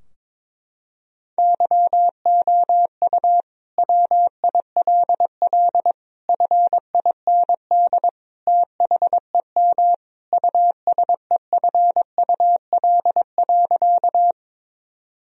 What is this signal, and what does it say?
Morse code, heard as a keyed tone, about 21 wpm, 700 Hz. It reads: YOU WILL FIND THEM USEFUL.
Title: Sentences from Top 1000 Words - Review Part 4 - 22wpm